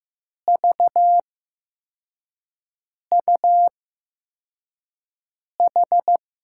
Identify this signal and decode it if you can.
Morse code, heard as a keyed tone, about 15 wpm, spaced as for 4 wpm, 700 Hz. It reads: VUH
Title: USA Call Signs - 6 Characters - 8x Character Spacing - 15wpm